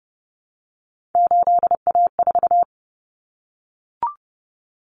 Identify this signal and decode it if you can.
Morse code, heard as a keyed tone, about 30 wpm, 700 Hz. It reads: 8A4
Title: Common Cut Numbers - 3-Digit 30wpm